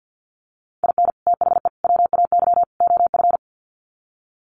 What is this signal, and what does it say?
Morse code, heard as a keyed tone, about 50 wpm, 700 Hz. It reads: IN THE WAY OF
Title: Sets of 4 Words 50wpm